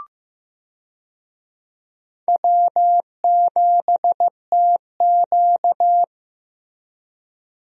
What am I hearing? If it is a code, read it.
Morse code, heard as a keyed tone, about 16 wpm, 700 Hz. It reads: W7TQ
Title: USA Call Signs - 4 Characters - Repeated 3x - 15wpm